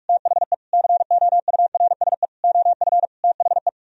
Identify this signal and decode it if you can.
Morse code, heard as a keyed tone, about 47 wpm, 700 Hz. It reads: THE COURSE OF THE